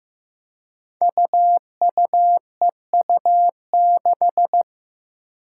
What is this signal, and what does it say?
Morse code, heard as a keyed tone, about 15 wpm, 700 Hz. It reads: UUEU6
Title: Five Challenging Letter-Number Combinations 15wpm